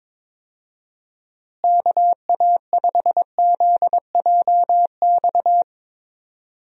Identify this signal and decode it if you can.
Morse code, heard as a keyed tone, about 22 wpm, 700 Hz. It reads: KA5ZJX
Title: USA Call Signs - 6 Characters - Repeated 3x - 22wpm